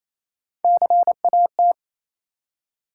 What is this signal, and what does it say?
Morse code, heard as a keyed tone, about 28 wpm, 700 Hz. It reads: CAT
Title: Three Letter Words - 28wpm